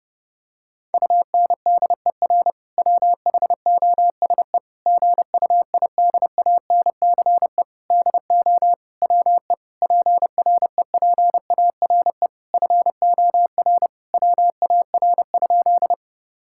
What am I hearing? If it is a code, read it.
Morse code, heard as a keyed tone, about 30 wpm, 700 Hz. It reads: UNDER WHOSE GUIDANCE DO WE PREPARE FOR WAR?
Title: Sentences from Top 1600 Words 30wpm